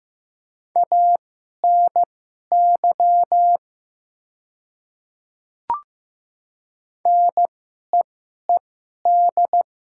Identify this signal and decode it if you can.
Morse code, heard as a keyed tone, about 15 wpm, spaced as for 11 wpm, 690 Hz. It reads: ANY NEED
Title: 2x Character Spacing - Top 300 Words - 15wpm